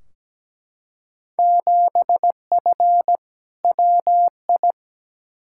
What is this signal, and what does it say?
Morse code, heard as a keyed tone, about 17 wpm, 700 Hz. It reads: 7F WI
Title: Field Day Exchanges Encore - 17wpm